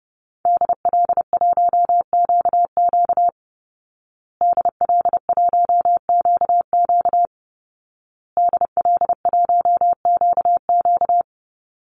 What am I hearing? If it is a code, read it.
Morse code, heard as a keyed tone, about 30 wpm, 700 Hz. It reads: DL1QQ DL1QQ DL1QQ
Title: Call Signs Repeated 2x - 30wpm